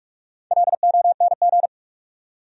Morse code, heard as a keyed tone, about 43 wpm, 700 Hz. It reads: LONG
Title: Top 100 Words 45wpm